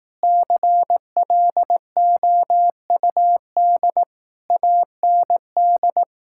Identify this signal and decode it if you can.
Morse code, heard as a keyed tone, about 18 wpm, 700 Hz. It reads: CLOUD AND
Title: Speed-Racing Common QSO Phrases Encore - 15wpm